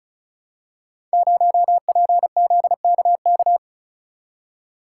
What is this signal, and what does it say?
Morse code, heard as a keyed tone, about 36 wpm, 700 Hz. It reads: 0PZKK
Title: Code Groups - 5 Letter:Numbers - 35wpm